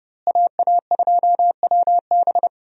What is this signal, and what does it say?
Morse code, heard as a keyed tone, about 31 wpm, 700 Hz. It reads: AA2WB
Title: USA Call Signs - 5 Characters - Repeated 3x - 30wpm